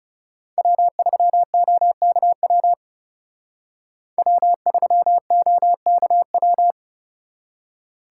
Morse code, heard as a keyed tone, about 38 wpm, 700 Hz. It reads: W3OKW W3OKW